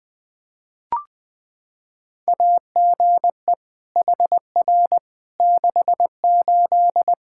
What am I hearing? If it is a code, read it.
Morse code, heard as a keyed tone, about 20 wpm, 700 Hz. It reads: AGE HR 68
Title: K1USN SST QSO Practice 20wpm